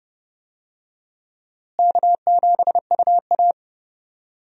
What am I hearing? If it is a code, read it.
Morse code, heard as a keyed tone, about 30 wpm, 700 Hz. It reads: K7UA
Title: USA Call Signs - 4 Characters - Repeated 3x - 30wpm